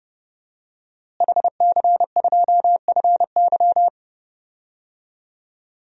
Morse code, heard as a keyed tone, about 30 wpm, 700 Hz. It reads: HC2FY
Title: DXCC-100 Call Signs - 5 Characters - Repeated 2x - 30wpm